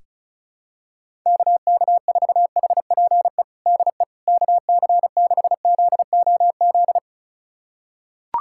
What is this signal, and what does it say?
Morse code, heard as a keyed tone, about 34 wpm, 700 Hz. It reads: KK4HPE DE KC6ZOZ
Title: USA Callsign DE Callsign - 6 Characters - 35wpm